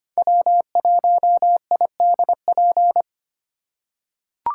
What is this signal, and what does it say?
Morse code, heard as a keyed tone, about 25 wpm, 700 Hz. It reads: W1IDP
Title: USA Call Signs - 5 Characters - 4x Character Spacing - 25wpm